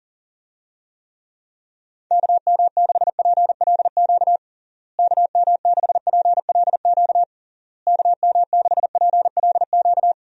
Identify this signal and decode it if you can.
Morse code, heard as a keyed tone, about 40 wpm, 700 Hz. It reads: KM6PLQ KM6PLQ KM6PLQ
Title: USA Call Signs - 6 Characters - Repeated 3x - 40wpm